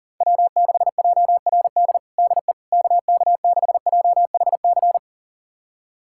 Morse code, heard as a keyed tone, about 40 wpm, 700 Hz. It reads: W6JRD DE KK6JHC
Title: USA Callsign DE Callsign 40wpm